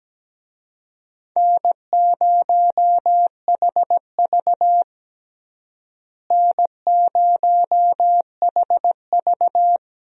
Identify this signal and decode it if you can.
Morse code, heard as a keyed tone, about 17 wpm, 700 Hz. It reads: N0HV N0HV
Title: USA Call Signs - 4 Characters - Repeated 3x - 17wpm